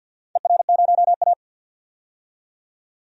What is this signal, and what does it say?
Morse code, heard as a keyed tone, about 47 wpm, 700 Hz. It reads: ER0A
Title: DXCC-100 Call Signs - 4 Characters - 50wpm